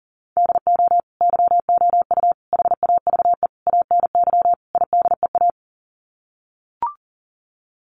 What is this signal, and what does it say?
Morse code, heard as a keyed tone, about 40 wpm, 700 Hz. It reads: DO YOU HAVE ANY IDEA
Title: Sets of 5 Words Encore 40wpm